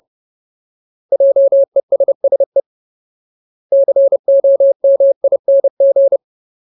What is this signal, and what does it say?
Morse code, heard as a keyed tone, about 30 wpm, 550 Hz. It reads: JESSE COMING
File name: Ultimate Word List / 30WPM Ultimate